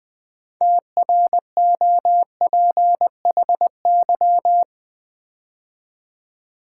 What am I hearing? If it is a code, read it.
Morse code, heard as a keyed tone, about 20 wpm, 700 Hz. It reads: TROPHY
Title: Build A Word - 3 Letters At A Time - 6-Letter Words - 20wpm